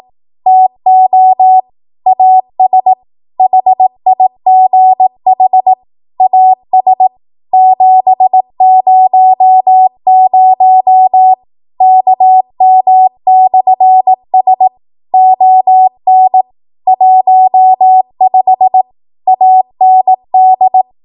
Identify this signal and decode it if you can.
Morse code, heard as a keyed tone, about 18 wpm, 750 Hz. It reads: TO AS HIGH AS 700 KM/S ON 15 AND
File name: Morse / 18WPM